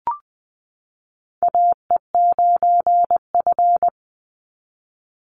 Morse code, heard as a keyed tone, about 20 wpm, 700 Hz. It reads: AE9F
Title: NA SOTA Chaser Callsigns 20wpm